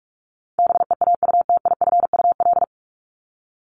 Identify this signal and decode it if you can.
Morse code, heard as a keyed tone, about 44 wpm, 700 Hz. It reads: BEAUTIFUL
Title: Top 751-1000 Words 45wpm